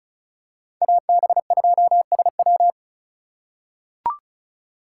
Morse code, heard as a keyed tone, about 36 wpm, 700 Hz. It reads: AB2SW
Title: USA Call Signs - 3x Character Spacing - 35wpm